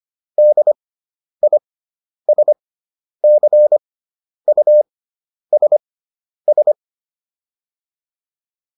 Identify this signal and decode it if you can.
Morse code, heard as a keyed tone, about 25 wpm, spaced as for 10 wpm, 600 Hz. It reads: DISCUSS